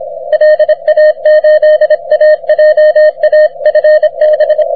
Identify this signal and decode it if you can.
Morse code, heard as a keyed tone, about 26 wpm, 610 Hz. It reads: LA8AJA FB
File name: wp / vq9jc 30m cw